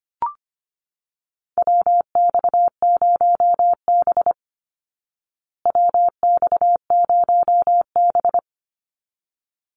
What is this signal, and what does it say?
Morse code, heard as a keyed tone, about 25 wpm, 700 Hz. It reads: WX0B WX0B
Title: Call Signs Repeated 3x - 25wpm